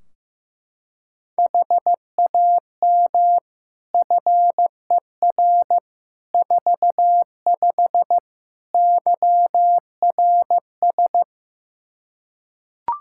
Morse code, heard as a keyed tone, about 15 wpm, 700 Hz. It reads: HAM FER 45 YRS
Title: Practice QSO - Exchange 3 - More Info - 15wpm